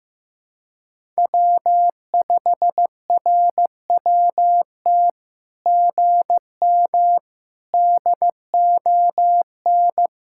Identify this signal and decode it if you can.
Morse code, heard as a keyed tone, about 15 wpm, 700 Hz. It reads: W5RWT GM DON
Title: SOTA QSO Practice 15wpm